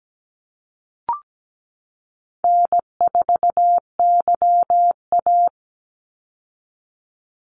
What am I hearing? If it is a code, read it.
Morse code, heard as a keyed tone, about 17 wpm, 700 Hz. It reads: N4YA
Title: Call Signs Repeated 2x - 17wpm